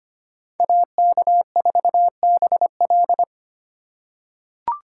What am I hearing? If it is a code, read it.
Morse code, heard as a keyed tone, about 25 wpm, 700 Hz. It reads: AK4BL
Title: USA Call Signs - 5 Characters - 4x Character Spacing - 25wpm